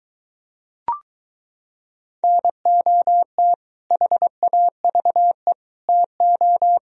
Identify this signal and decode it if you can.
Morse code, heard as a keyed tone, about 23 wpm, 700 Hz. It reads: NOT HAVE TO